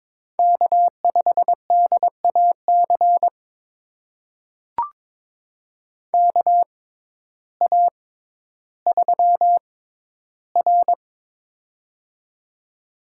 Morse code, heard as a keyed tone, about 23 wpm, spaced as for 12 wpm, 700 Hz. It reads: K5DAC KA3R
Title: USA Call Signs - 6x Character Spacing - 22wpm